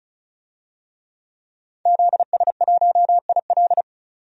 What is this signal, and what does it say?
Morse code, heard as a keyed tone, about 34 wpm, 700 Hz. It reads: ZS1IL
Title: DXCC-100 Call Signs - 5 Characters - 35wpm